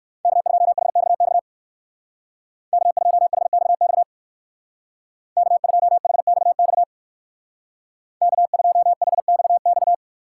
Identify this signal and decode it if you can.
Morse code, heard as a keyed tone, about 59 wpm, 700 Hz. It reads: K2HXX K2HXX K2HXX K2HXX